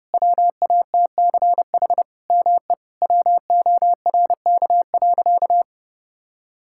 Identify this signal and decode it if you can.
Morse code, heard as a keyed tone, about 32 wpm, 700 Hz. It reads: WATCH ME WORK.
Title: Sentences from Top 1600 Words 30wpm